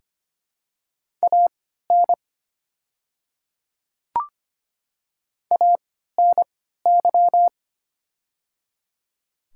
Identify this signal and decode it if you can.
Morse code, heard as a keyed tone, about 25 wpm, spaced as for 14 wpm, 700 Hz. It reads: AN ANY
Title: 3x Character Spacing - Top 200 Words - 25wpm